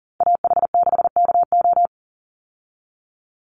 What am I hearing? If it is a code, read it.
Morse code, heard as a keyed tone, about 40 wpm, 700 Hz. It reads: AH6KO
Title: Call Signs Repeated 3x - 40wpm